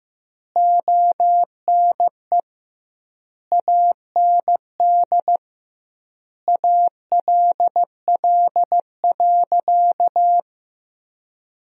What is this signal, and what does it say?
Morse code, heard as a keyed tone, about 15 wpm, 700 Hz. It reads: ONE AND ALL.
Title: 2x Word-Spacing - Sentences from Top 200 Words - 15wpm